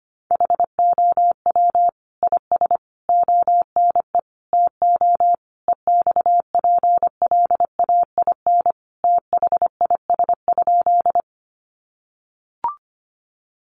Morse code, heard as a keyed tone, about 25 wpm, 700 Hz. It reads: HOW IS ONE TO EXPLAIN THIS?
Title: Sentences from Top 1100 Words 25wpm